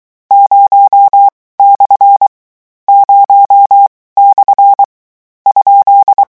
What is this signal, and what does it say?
Morse code, heard as a keyed tone, about 23 wpm, 800 Hz. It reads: 0 / 0 / ?
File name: morse mp3 / lecon13-vitesse 22